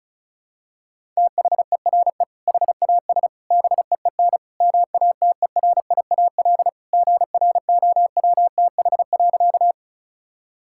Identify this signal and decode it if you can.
Morse code, heard as a keyed tone, about 36 wpm, 700 Hz. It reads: THERE HAS BEEN MATERIAL GROWTH.
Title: Sentences from Top 1000 Words - Review Part 5 - 35wpm